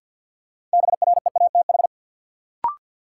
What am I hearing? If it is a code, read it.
Morse code, heard as a keyed tone, about 47 wpm, 700 Hz. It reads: BREATH
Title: Sets of 5 Words Encore - New Word Drill - 50wpm